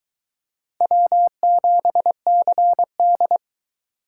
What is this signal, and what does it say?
Morse code, heard as a keyed tone, about 22 wpm, 700 Hz. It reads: W7CD